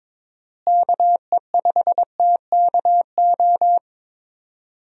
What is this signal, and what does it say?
Morse code, heard as a keyed tone, about 22 wpm, 700 Hz. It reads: KE5TKO
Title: USA Call Signs - 6 Characters - Repeated 2x - 22wpm